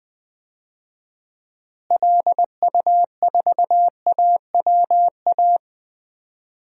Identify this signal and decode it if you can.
Morse code, heard as a keyed tone, about 20 wpm, 700 Hz. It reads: LU4AWA
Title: DXCC-100 Call Signs - 6 Characters - Repeated 3x - 20wpm